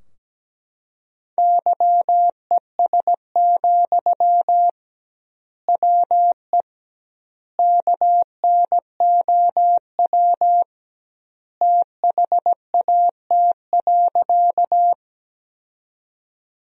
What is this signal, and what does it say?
Morse code, heard as a keyed tone, about 17 wpm, 700 Hz. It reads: YES, WE KNOW THAT.
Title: 2x Word-Spacing - Sentences from Top 500 Words - 17wpm